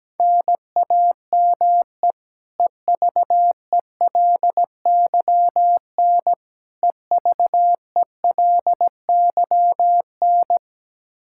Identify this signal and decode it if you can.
Morse code, heard as a keyed tone, about 17 wpm, 700 Hz. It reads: NAME EVELYN EVELYN
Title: Practice QSO - Exchange 1 - Beginning - 17wpm